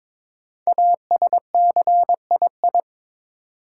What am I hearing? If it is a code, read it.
Morse code, heard as a keyed tone, about 22 wpm, 700 Hz. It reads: ASCII
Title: Sets of 5 Words - New Word Drill - 22wpm